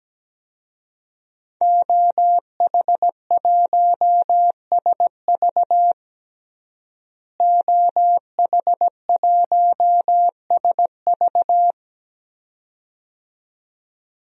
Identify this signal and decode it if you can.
Morse code, heard as a keyed tone, about 17 wpm, 700 Hz. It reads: OH1SV OH1SV
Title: DXCC-100 Call Signs - 5 Characters - Repeated 2x - 17wpm